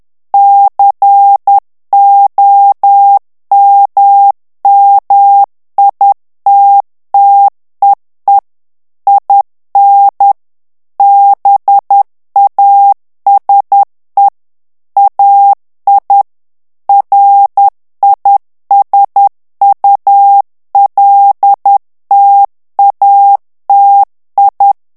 Morse code, heard as a keyed tone, about 10 wpm, 800 Hz. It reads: COMMITTEE IN BASE AI RISULTATI
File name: ITALIEN texte clair 2 vitesse 10 mots minute